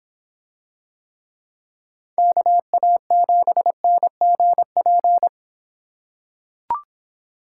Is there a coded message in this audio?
Yes: Morse code from 2.2 to 5.3 s, heard as a keyed tone, about 26 wpm, 700 Hz. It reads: KA7NGP